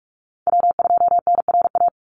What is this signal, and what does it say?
Morse code, heard as a keyed tone, about 44 wpm, 700 Hz. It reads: W2NRA
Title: Call Signs - 45wpm